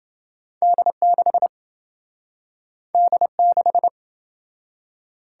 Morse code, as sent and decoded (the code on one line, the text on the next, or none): -.. -.... / -.. -....
D6 D6